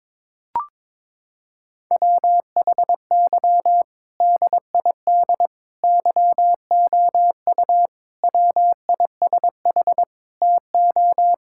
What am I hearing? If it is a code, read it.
Morse code, heard as a keyed tone, about 22 wpm, 700 Hz. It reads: WHY DID YOU WISH TO
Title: Sentences from Top 1600 Words 22wpm